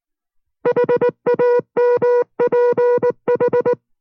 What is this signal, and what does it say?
Morse code, heard as a keyed tone, about 19 wpm, 490 Hz. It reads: HAMPH